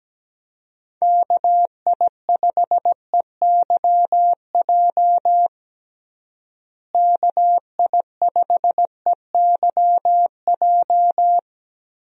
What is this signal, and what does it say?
Morse code, heard as a keyed tone, about 17 wpm, 700 Hz. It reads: KI5EYJ KI5EYJ
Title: USA Call Signs - 6 Characters - Repeated 2x - 17wpm